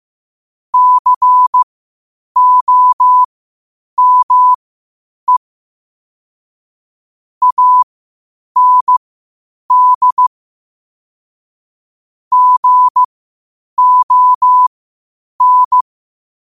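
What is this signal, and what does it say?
Morse code, heard as a keyed tone, about 15 wpm, spaced as for 8 wpm, 1000 Hz. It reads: COME AND GON
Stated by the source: Morse practice at 8 WPM from Morse Resource.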